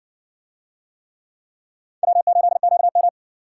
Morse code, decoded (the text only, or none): W8QM